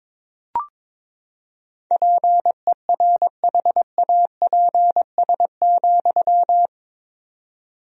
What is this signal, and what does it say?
Morse code, heard as a keyed tone, about 22 wpm, 700 Hz. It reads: PERHAPS,
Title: 4x Word-Spacing - Sentences from Top 500 Words - 22wpm